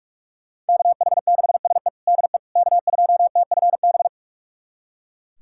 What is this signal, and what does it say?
Morse code, heard as a keyed tone, about 42 wpm, 700 Hz. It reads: KH6SE DE K2TFB
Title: USA Callsign DE Callsign - 5 Characters - 45wpm